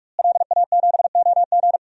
Morse code, heard as a keyed tone, about 46 wpm, 700 Hz. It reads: LA7OG
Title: DXCC-100 Call Signs - 5 Characters - Repeated 3x - 45wpm